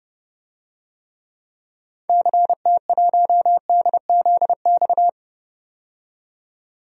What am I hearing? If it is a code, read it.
Morse code, heard as a keyed tone, about 30 wpm, 700 Hz. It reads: CT1DZX
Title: DXCC-100 Call Signs - 6 Characters - 30wpm